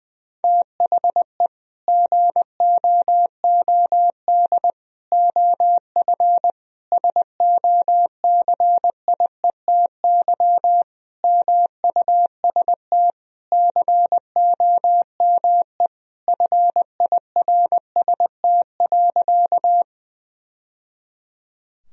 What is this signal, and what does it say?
Morse code, heard as a keyed tone, about 21 wpm, 700 Hz. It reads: THE GOOD OF SOCIETY MUST COME FIRST.
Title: Sentences from Top 1000 Words - Review Part 4 - 20wpm